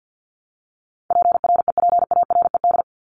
Morse code, heard as a keyed tone, about 50 wpm, 700 Hz. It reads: PREPARED
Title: ED Suffix - Top 100 Words with Base Words 50wpm